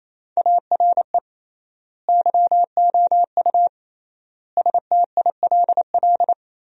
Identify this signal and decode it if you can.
Morse code, heard as a keyed tone, about 28 wpm, 700 Hz. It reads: ARE YOU STILL
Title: 3x Word-Spacing - Sentences from Top 400 Words - 28wpm